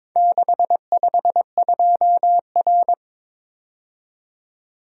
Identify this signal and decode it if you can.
Morse code, heard as a keyed tone, about 22 wpm, 700 Hz. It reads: 652R
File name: practice / 22wpm / Four Letter-Number Combinations - 22wpm